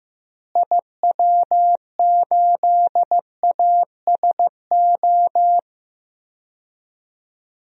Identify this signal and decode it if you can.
Morse code, heard as a keyed tone, about 15 wpm, 700 Hz. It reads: IW8ASO
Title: DXCC-100 Call Signs - 6 Characters - Repeated 2x - 15wpm